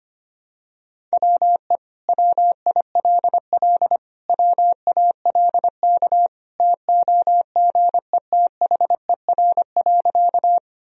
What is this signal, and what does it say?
Morse code, heard as a keyed tone, about 25 wpm, 700 Hz. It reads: WE WILL WALK TOGETHER.
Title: Sentences from Top 1000 Words - Review Part 1 - 25wpm